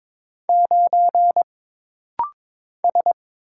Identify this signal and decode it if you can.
Morse code, heard as a keyed tone, about 22 wpm, 700 Hz. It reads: 9 S
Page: This rapid-fire timing is geared to push your ICR (Instant Character Recognition) skills.